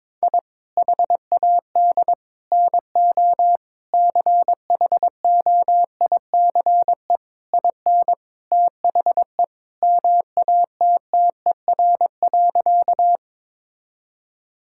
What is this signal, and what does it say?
Morse code, heard as a keyed tone, about 22 wpm, 700 Hz. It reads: I HAD NO CHOICE IN THE MATTER.
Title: Sentences from Top 1000 Words - Review Part 6 - 22wpm